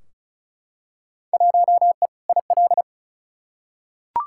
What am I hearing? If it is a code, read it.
Morse code, heard as a keyed tone, about 36 wpm, 700 Hz. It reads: 1E IL
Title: Field Day Exchanges - 35wpm